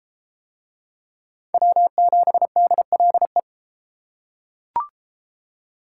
Morse code, heard as a keyed tone, about 29 wpm, 700 Hz. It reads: W7DLE